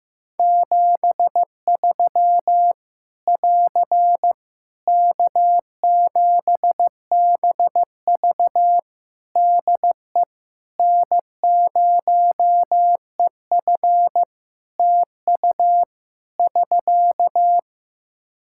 Morse code, heard as a keyed tone, about 15 wpm, 700 Hz. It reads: 73 <AR> K7BV DE N0EF TU <SK>
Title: Practice QSO - Exchange 4 - Ending - 15wpm